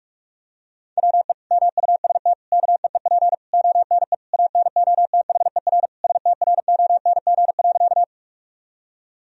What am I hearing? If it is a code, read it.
Morse code, heard as a keyed tone, about 46 wpm, 700 Hz. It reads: WE MUST KEEP ONE ANOTHER STRONG.
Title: Sentences from Top 1000 Words - Review Part 3 - 45wpm